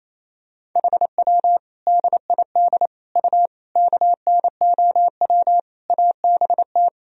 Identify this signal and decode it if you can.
Morse code, heard as a keyed tone, about 28 wpm, 700 Hz. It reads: HW DID U KNOW ABT
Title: Sentences from Top 100 Words and Abbreviations 28wpm